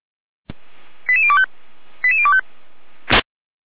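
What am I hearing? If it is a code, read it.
Five-tone selective call (ZVEI), heard as quick runs of five tones, tones of about 70 ms. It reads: 88025 88025
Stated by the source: Five-tone ZVEI selective call received by SDR, from a German fire brigade.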